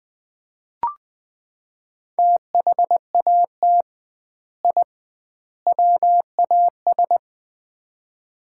Morse code, heard as a keyed tone, about 21 wpm, 700 Hz. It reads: THAT I WAS
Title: 2x Word-Spacing - Sets of 3 Words - 20wpm